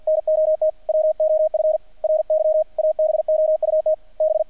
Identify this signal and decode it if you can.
Morse code, heard as a keyed tone, about 50 wpm, 620 Hz, with radio noise. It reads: NOT WOF RY ABOUT B
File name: archived / CRSnet